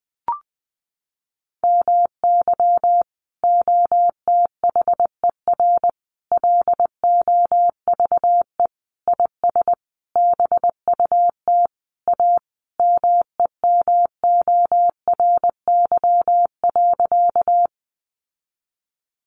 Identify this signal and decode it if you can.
Morse code, heard as a keyed tone, about 20 wpm, 700 Hz. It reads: MY OTHER LOVE IS BUT A MEMORY.
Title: Sentences from Top 1300 Words 20wpm